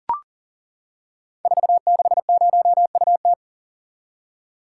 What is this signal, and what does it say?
Morse code, heard as a keyed tone, about 38 wpm, 690 Hz. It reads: 460UT
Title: Code Groups - 5 Letter:Numbers - 40wpm